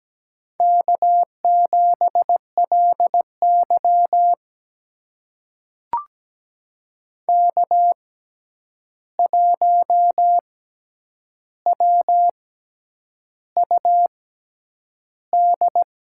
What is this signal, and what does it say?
Morse code, heard as a keyed tone, about 17 wpm, spaced as for 8 wpm, 700 Hz. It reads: K7LY K1WUD
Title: USA Call Signs - 6x Character Spacing - 17wpm